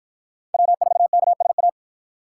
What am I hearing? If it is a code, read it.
Morse code, heard as a keyed tone, about 52 wpm, 700 Hz. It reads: W4KSA